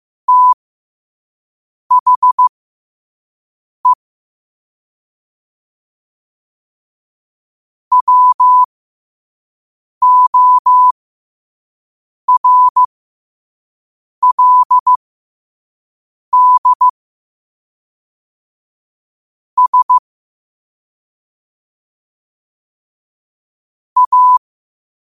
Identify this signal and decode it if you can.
Morse code, heard as a keyed tone, about 15 wpm, spaced as for 5 wpm, 1000 Hz. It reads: THE WORLD S A
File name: qotdfiles / 5wpm